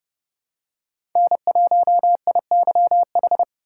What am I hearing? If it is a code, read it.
Morse code, heard as a keyed tone, about 30 wpm, 700 Hz. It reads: N1IYH
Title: USA Call Signs - 4x Character Spacing - 30wpm